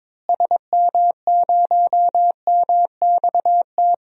Morse code, heard as a keyed tone, about 22 wpm, 700 Hz. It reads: SM0MXT
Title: DXCC-100 Call Signs - 6 Characters - Repeated 3x - 22wpm